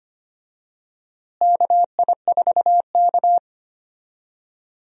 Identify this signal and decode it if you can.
Morse code, heard as a keyed tone, about 25 wpm, 700 Hz. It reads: KI4K
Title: USA Call Signs - 4 Characters - Repeated 3x - 25wpm